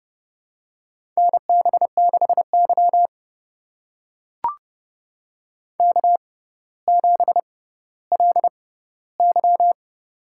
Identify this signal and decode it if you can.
Morse code, heard as a keyed tone, about 29 wpm, spaced as for 15 wpm, 700 Hz. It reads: NB6Y K7LY
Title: USA Call Signs - 4 Characters - 6x Character Spacing - 30wpm